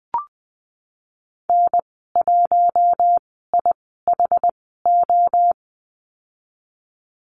Morse code, heard as a keyed tone, about 20 wpm, 700 Hz. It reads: N 1 I H O
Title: 076 Lesson 13g - Letter L - Call Signs 20wpm